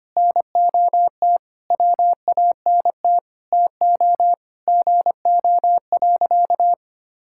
Morse code, heard as a keyed tone, about 25 wpm, 700 Hz. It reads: NOT WANT TO GO.
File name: practice / 25wpm / Sentences with 4-Word Phrases - 25wpm